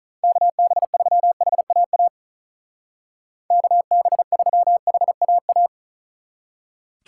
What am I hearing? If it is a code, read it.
Morse code, heard as a keyed tone, about 44 wpm, 700 Hz. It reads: KB3HAA KB3HAA
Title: Speed Racing - USA 2x3 Call Signs 35wpm